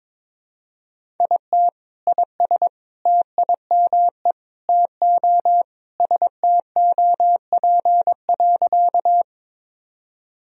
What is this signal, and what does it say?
Morse code, heard as a keyed tone, about 21 wpm, 700 Hz. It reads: IT IS TIME TO STOP.
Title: Sentences from Top 1000 Words - Review Part 1 - 22wpm